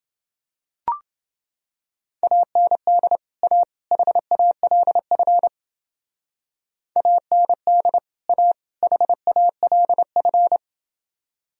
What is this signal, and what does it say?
Morse code, heard as a keyed tone, about 30 wpm, 700 Hz. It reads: AND A HALF AND A HALF